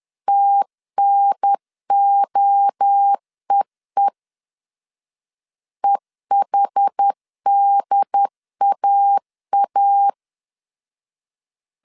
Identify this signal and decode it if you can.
Morse code, heard as a keyed tone, about 10 wpm, 800 Hz. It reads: TNOEE EHDAA
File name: CW / DET / 1 3 1500C S C 10wpm 3335